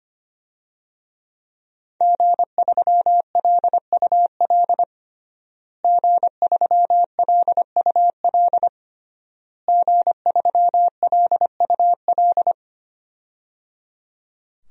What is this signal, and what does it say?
Morse code, heard as a keyed tone, about 25 wpm, 700 Hz. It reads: G3LUL G3LUL G3LUL
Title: DXCC-100 Call Signs - 5 Characters - Repeated 3x - 25wpm